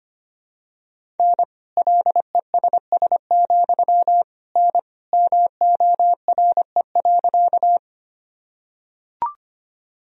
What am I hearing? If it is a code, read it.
Morse code, heard as a keyed tone, about 25 wpm, 700 Hz. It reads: N LESS, N MORE.
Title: Sentences from Top 400 Words and Abbreviations 25wpm